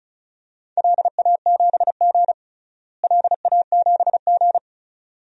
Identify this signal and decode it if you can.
Morse code, heard as a keyed tone, about 34 wpm, 700 Hz. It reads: LA7G LA7G
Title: DXCC-100 Call Signs - 4 Characters - Repeated 2x - 35wpm